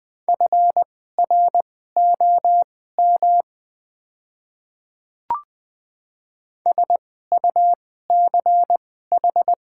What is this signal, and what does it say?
Morse code, heard as a keyed tone, about 20 wpm, spaced as for 14 wpm, 700 Hz. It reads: FROM SUCH
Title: 2x Character Spacing - Top 300 Words - 20wpm